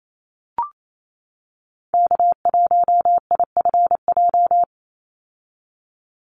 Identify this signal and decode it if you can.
Morse code, heard as a keyed tone, about 28 wpm, 700 Hz. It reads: K1IFJ
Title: Call Signs - 28wpm